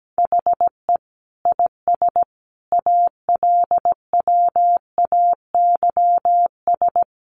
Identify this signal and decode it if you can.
Morse code, heard as a keyed tone, about 17 wpm, 700 Hz. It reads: HE IS ALWAYS
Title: Sentences from Top 300 Words 17wpm